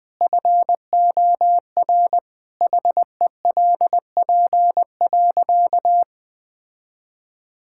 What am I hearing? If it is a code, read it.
Morse code, heard as a keyed tone, about 20 wpm, 700 Hz. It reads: FOR HELP.
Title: Sentences with 2-Word Phrases - 20wpm